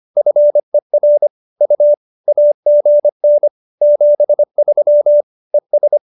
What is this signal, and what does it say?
Morse code, heard as a keyed tone, about 25 wpm, 590 Hz. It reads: FER U AGN 73 ES